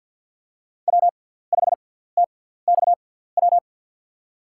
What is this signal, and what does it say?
Morse code, heard as a keyed tone, about 48 wpm, spaced as for 17 wpm, 700 Hz. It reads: W5TXW
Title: USA Call Signs - 5 Characters - 6x Character Spacing - 50wpm